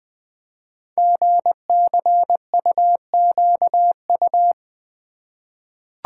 Morse code, heard as a keyed tone, about 20 wpm, 700 Hz. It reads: GCUQU
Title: Code Groups - 5 Letters - 20wpm